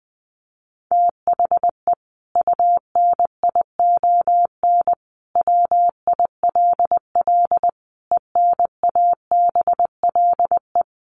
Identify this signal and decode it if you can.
Morse code, heard as a keyed tone, about 20 wpm, 700 Hz. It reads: THE UNION WILL ENABLE
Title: Sentences from Top 1500 Words 20wpm